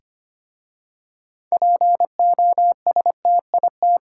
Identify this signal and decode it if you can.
Morse code, heard as a keyed tone, about 25 wpm, 700 Hz. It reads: POSTIT
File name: practice / 25wpm / Difficult Words with Hints - 25wpm